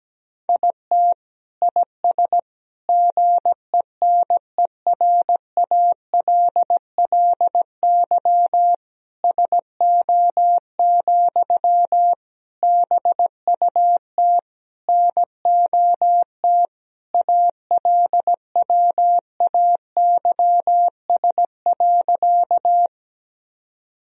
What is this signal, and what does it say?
Morse code, heard as a keyed tone, about 17 wpm, 700 Hz. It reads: IT IS GENERALLY SO, BUT NOT ALWAYS.
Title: Sentences from Top 1000 Words - Review Part 4 - 17wpm